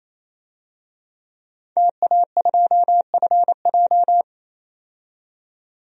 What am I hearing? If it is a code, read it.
Morse code, heard as a keyed tone, about 28 wpm, 700 Hz. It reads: TA2FJ
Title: DXCC-100 Call Signs - 5 Characters - 28wpm